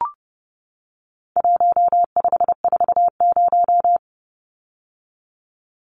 Morse code, heard as a keyed tone, about 30 wpm, 700 Hz. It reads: 1540
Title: Numbers - 4-Digit 30wpm